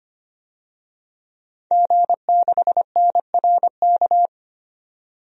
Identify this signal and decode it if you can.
Morse code, heard as a keyed tone, about 25 wpm, 700 Hz. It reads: G6NRK
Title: DXCC-100 Call Signs - 5 Characters - 25wpm